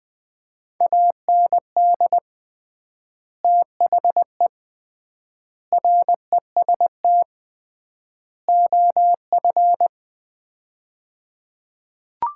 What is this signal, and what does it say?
Morse code, heard as a keyed tone, about 20 wpm, 700 Hz. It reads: AND THE REST OF
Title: 3x Word-Spacing - Sets of 4 Words - 20wpm